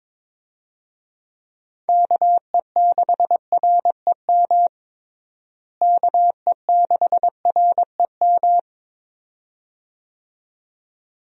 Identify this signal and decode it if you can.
Morse code, heard as a keyed tone, about 22 wpm, 700 Hz. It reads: KE6REM KE6REM
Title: USA Call Signs - 6 Characters - Repeated 2x - 22wpm